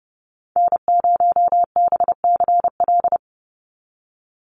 Morse code, heard as a keyed tone, about 30 wpm, 700 Hz. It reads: N0BCL
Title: USA Call Signs 30wpm